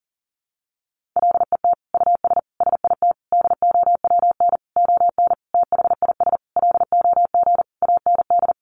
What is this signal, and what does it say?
Morse code, heard as a keyed tone, about 40 wpm, 700 Hz. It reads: LET US SIT DOWN ON THIS LOG AND
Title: Sentences from Top 1400 Words 40wpm